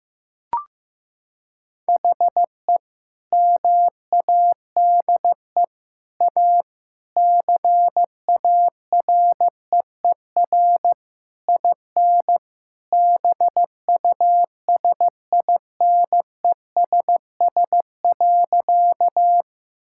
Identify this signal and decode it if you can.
Morse code, heard as a keyed tone, about 15 wpm, 700 Hz. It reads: HE MADE A CAREER IN BUSINESS.
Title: Sentences from Top 1000 Words - Review Part 6 - 15wpm